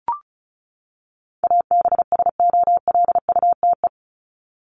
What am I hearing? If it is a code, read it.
Morse code, heard as a keyed tone, about 34 wpm, 700 Hz. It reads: ABSOLUTE
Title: LY Suffix - Top 100 Words with Base Words 35wpm